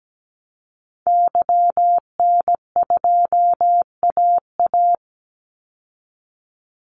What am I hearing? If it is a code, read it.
Morse code, heard as a keyed tone, about 17 wpm, 700 Hz. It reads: YN2AA
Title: Call Signs Repeated 3x - 17wpm